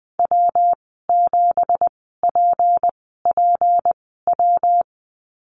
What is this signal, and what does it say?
Morse code, heard as a keyed tone, about 20 wpm, spaced as for 14 wpm, 700 Hz. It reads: W7PPW
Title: 186 Lesson 30g - Letter b - Call Signs 20wpm